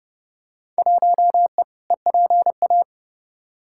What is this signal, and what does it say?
Morse code, heard as a keyed tone, about 30 wpm, 700 Hz. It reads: 1E EPA
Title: Field Day Exchanges - 30wpm